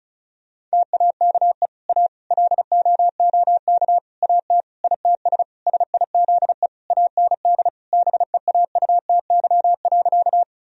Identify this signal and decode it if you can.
Morse code, heard as a keyed tone, about 36 wpm, 700 Hz. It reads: TAKE A LOOK AT ITS SIZE AND BEAUTY.
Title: Sentences from Top 1000 Words - Review Part 2 - 35wpm